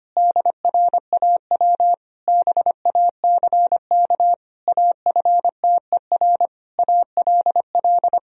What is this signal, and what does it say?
Morse code, heard as a keyed tone, about 25 wpm, 700 Hz. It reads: DRAW BACK AFTER ALL
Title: Sentences from Top 1000 Words - Review Part 1 - 25wpm